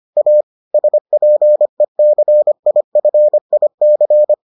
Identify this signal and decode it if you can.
Morse code, heard as a keyed tone, about 25 wpm, 600 Hz. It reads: A SPECIFIC